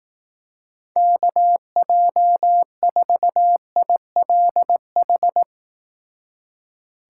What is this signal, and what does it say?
Morse code, heard as a keyed tone, about 18 wpm, 700 Hz. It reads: KJ4ILH
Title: Speed Racing - USA 2x3 Call Signs 15wpm